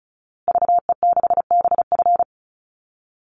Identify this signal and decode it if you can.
Morse code, heard as a keyed tone, about 34 wpm, 700 Hz. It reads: VE6BF
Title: Call Signs Repeated 2x - 35wpm